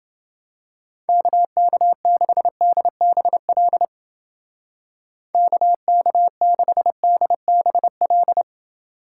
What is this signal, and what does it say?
Morse code, heard as a keyed tone, about 29 wpm, 700 Hz. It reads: KK6DBL KK6DBL